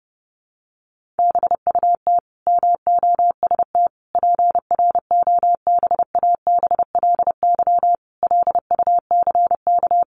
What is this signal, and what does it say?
Morse code, heard as a keyed tone, about 30 wpm, 700 Hz. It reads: BUT MOST PROBABLY LUCK
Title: Sentences from Top 1500 Words 30wpm